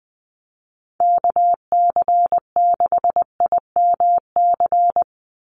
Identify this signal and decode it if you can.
Morse code, heard as a keyed tone, about 20 wpm, 700 Hz. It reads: KC6IMC
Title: USA Call Signs 20wpm